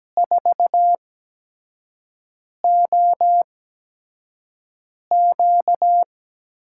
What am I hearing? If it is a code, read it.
Morse code, heard as a keyed tone, about 17 wpm, spaced as for 5 wpm, 700 Hz. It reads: 4OQ
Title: USA Call Signs - 5 Characters - 8x Character Spacing - 17wpm